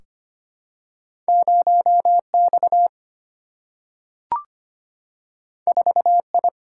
Morse code, heard as a keyed tone, about 25 wpm, 700 Hz. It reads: 0X 4I
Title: Single-Letter Single-Number 25wpm